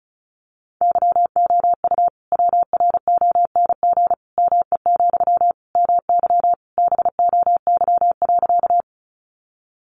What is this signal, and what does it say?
Morse code, heard as a keyed tone, about 34 wpm, 700 Hz. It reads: YOU WRONG ME, MY BOY.
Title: Sentences from Top 900 Words 35wpm